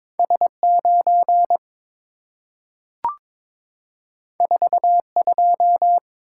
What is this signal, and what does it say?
Morse code, heard as a keyed tone, about 22 wpm, 690 Hz. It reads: S9 42
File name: practice / 22wpm / Two Letter-Number Combinations - 22wpm